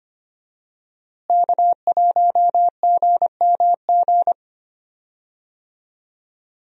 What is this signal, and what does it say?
Morse code, heard as a keyed tone, about 25 wpm, 700 Hz. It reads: K1GMG
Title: USA Call Signs - 5 Characters - Repeated 2x - 25wpm